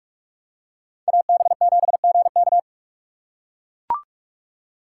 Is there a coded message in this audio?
Yes: Morse code from 1.1 to 2.6 s, heard as a keyed tone, about 44 wpm, 700 Hz. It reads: AB7GK